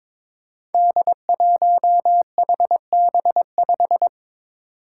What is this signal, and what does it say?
Morse code, heard as a keyed tone, about 22 wpm, 700 Hz. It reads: D1HB5
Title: Five Challenging Letter-Number Combinations 22wpm